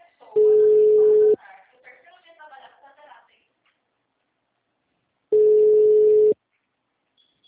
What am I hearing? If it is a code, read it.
Ringback tone (Europe, 425 Hz), with speech.